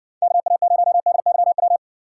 Morse code, heard as a keyed tone, about 58 wpm, 700 Hz. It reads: KA0DYW